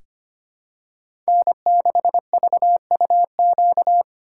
Morse code, heard as a keyed tone, about 25 wpm, 700 Hz. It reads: N6VUQ
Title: USA Call Signs - 2x Character Spacing - 25wpm